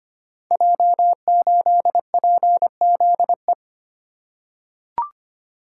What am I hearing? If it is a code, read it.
Morse code, heard as a keyed tone, about 25 wpm, 700 Hz. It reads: J8PZE
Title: Code Groups - 5 Letter:Numbers - 25wpm